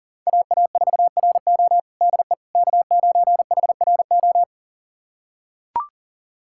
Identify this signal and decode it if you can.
Morse code, heard as a keyed tone, about 40 wpm, 700 Hz. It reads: AA4RO DE K9HRO
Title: USA Callsign DE Callsign - 5 Characters - 40wpm